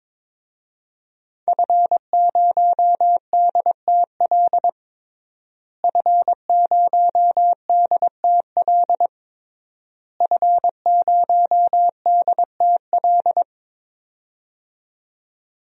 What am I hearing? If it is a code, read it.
Morse code, heard as a keyed tone, about 22 wpm, 700 Hz. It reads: F0DTL F0DTL F0DTL
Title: DXCC-100 Call Signs - 5 Characters - Repeated 3x - 22wpm